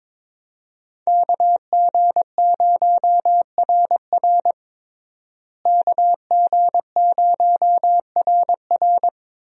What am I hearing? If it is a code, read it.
Morse code, heard as a keyed tone, about 22 wpm, 700 Hz. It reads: KG0RR KG0RR
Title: USA Call Signs - 5 Characters - Repeated 3x - 22wpm